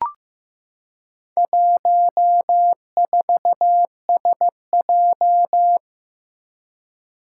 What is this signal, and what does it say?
Morse code, heard as a keyed tone, about 15 wpm, 700 Hz. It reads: 14SJ
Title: Four Challenging Letter-Number Combinations 15wpm